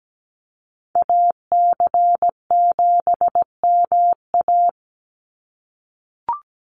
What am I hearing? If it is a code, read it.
Morse code, heard as a keyed tone, about 17 wpm, 700 Hz. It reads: AC7MA
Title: NA SOTA Activator Callsigns 17wpm